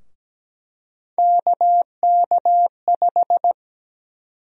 Morse code, heard as a keyed tone, about 17 wpm, 700 Hz. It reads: KK5
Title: USA Call Sign Prefixes - Repeated 2x - 17wpm